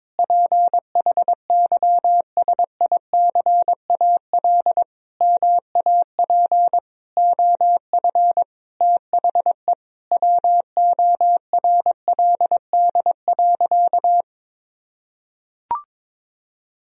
Morse code, heard as a keyed tone, about 22 wpm, 700 Hz. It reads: PHYSICAL MAP OF THE WORLD.
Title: Sentences from Top 1000 Words - Review Part 4 - 22wpm